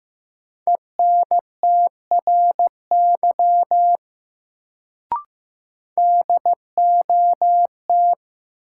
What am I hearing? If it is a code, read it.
Morse code, heard as a keyed tone, about 15 wpm, 700 Hz. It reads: ENTRY DOT
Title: Sentences from Top 1500 Words - New Word Drill Alt - 15wpm